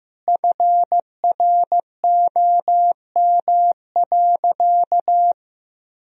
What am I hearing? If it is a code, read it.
Morse code, heard as a keyed tone, about 15 wpm, 700 Hz. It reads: FROM.
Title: Sentences from Top 1000 Words - Review Part 4 - 15wpm